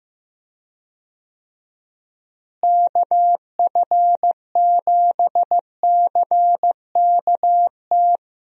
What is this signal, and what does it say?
Morse code, heard as a keyed tone, about 15 wpm, 700 Hz. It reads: KF7CKT
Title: USA Call Signs - 6 Characters - Repeated 2x - 15wpm